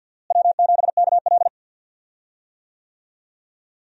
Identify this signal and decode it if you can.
Morse code, heard as a keyed tone, about 52 wpm, 700 Hz. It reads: W7KL
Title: USA Call Signs - 4 Characters - Repeated 2x - 50wpm